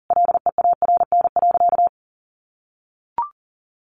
Morse code, heard as a keyed tone, about 40 wpm, 700 Hz. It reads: LEARN.
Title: Sentences from Top 800 Words 40wpm